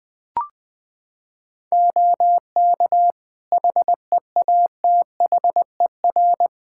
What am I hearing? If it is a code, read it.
Morse code, heard as a keyed tone, about 20 wpm, 700 Hz. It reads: OK HEATHER
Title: K1USN SST QSO Practice 20wpm